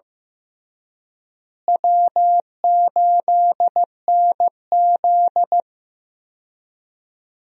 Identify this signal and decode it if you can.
Morse code, heard as a keyed tone, about 16 wpm, 700 Hz. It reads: W8NZ
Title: USA Call Signs - 4 Characters - Repeated 3x - 15wpm